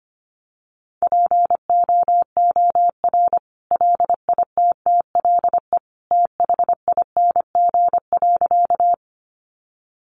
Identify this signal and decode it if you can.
Morse code, heard as a keyed tone, about 25 wpm, 700 Hz. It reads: POOR LITTLE THING.
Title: Sentences from Top 1000 Words 25wpm